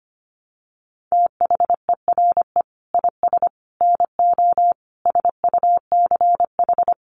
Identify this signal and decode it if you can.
Morse code, heard as a keyed tone, about 25 wpm, 700 Hz. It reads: THERE IS NO SUCH
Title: Sets of 5 Words Encore 25wpm